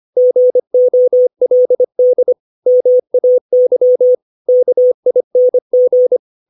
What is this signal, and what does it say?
Morse code, heard as a keyed tone, about 25 wpm, 500 Hz. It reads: GOLD MAY KING
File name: 632 words / 632 Words 25WPM